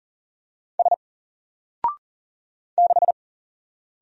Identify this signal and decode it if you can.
Morse code, heard as a keyed tone, about 40 wpm, 700 Hz. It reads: S 6
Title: Single Letter-Number Alt 40wpm